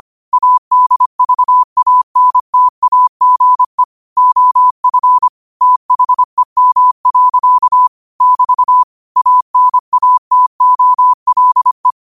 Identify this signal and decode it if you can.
Morse code, heard as a keyed tone, about 25 wpm, 1000 Hz. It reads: ADVANTAGE OF THEM. <BT> ANATOLE
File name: qotdfiles / 25wpm